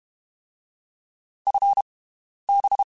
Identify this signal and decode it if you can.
Morse code, heard as a keyed tone, about 32 wpm, 800 Hz. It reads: F B
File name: (spaced) / QSOs Spaced-mp3s / WD0FBT 1 @30 sp